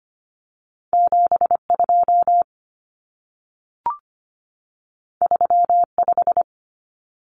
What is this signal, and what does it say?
Morse code, heard as a keyed tone, about 25 wpm, 700 Hz. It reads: 72 35
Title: Numbers - 2-Digit 25wpm